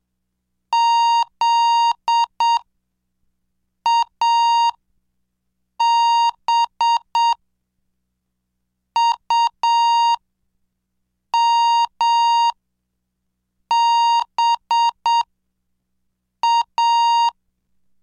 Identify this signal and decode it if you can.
Morse code, heard as a keyed tone, about 7 wpm, spaced as for 5 wpm, 930 Hz. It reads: ZABUMBA
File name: cursos / CDestudo / 17 Track 17